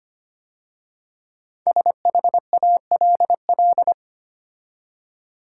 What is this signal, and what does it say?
Morse code, heard as a keyed tone, about 25 wpm, 700 Hz. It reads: SHALL
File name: 5 Letter Words - First 3 Letters Given - 25wpm